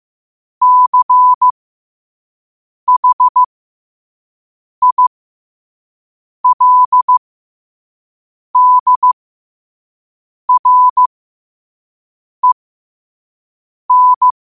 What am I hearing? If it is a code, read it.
Morse code, heard as a keyed tone, about 15 wpm, spaced as for 5 wpm, 1000 Hz. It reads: CHILDREN
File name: qotdfiles / 5wpm